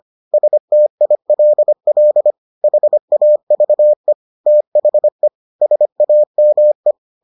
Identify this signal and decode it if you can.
Morse code, heard as a keyed tone, about 25 wpm, 600 Hz. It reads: STILL HAVE THE SAME